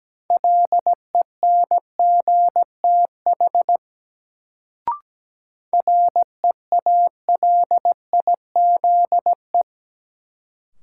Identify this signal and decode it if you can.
Morse code, heard as a keyed tone, about 17 wpm, 700 Hz. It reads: LENGTH REALIZE
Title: Sentences from Top 1000 Words - New Word Drill - 17wpm